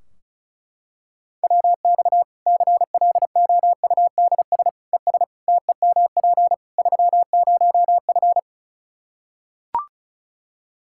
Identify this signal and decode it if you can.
Morse code, heard as a keyed tone, about 36 wpm, 700 Hz. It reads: WX CLOUDS ES TEMP 30F
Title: K1USN SST QSO Practice 35wpm